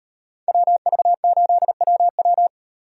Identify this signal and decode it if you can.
Morse code, heard as a keyed tone, about 38 wpm, 700 Hz. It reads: WV8WW